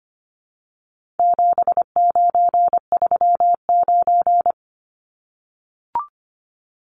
Morse code, heard as a keyed tone, about 25 wpm, 700 Hz. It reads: 7939
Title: Numbers - 4-Digit 25wpm